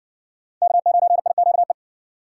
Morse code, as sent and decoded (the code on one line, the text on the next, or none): -.- ----. .. --.- .
K9IQE